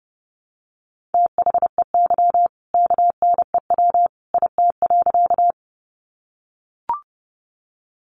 Morse code, heard as a keyed tone, about 30 wpm, 700 Hz. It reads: THEY KNEW IT.
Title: Sentences from Top 400 Words 30wpm